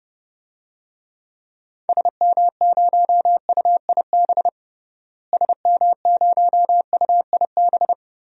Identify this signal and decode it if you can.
Morse code, heard as a keyed tone, about 30 wpm, 700 Hz. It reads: SM0UIB SM0UIB
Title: DXCC-100 Call Signs - 6 Characters - Repeated 3x - 30wpm